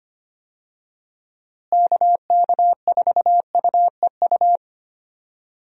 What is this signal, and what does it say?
Morse code, heard as a keyed tone, about 25 wpm, 700 Hz. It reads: KK4UEU
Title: USA Call Signs - 6 Characters - Repeated 2x - 25wpm